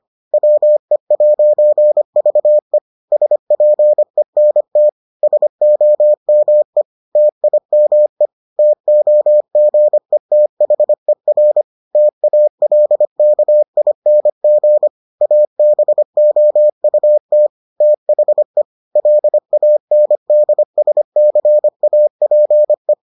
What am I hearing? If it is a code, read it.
Morse code, heard as a keyed tone, about 25 wpm, 600 Hz. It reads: WE'VE SPENT SOME TIME TOGETHER TALKING ABOUT THE LANDSCAPE